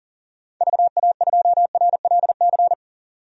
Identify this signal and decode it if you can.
Morse code, heard as a keyed tone, about 40 wpm, 700 Hz. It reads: VA2RLC
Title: DXCC-100 Call Signs - 6 Characters - 40wpm